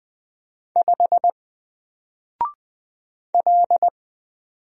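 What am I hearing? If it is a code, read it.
Morse code, heard as a keyed tone, about 20 wpm, 700 Hz. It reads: 5 L
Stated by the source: Single letter and number Morse drill at 20 wpm, from N4LCD.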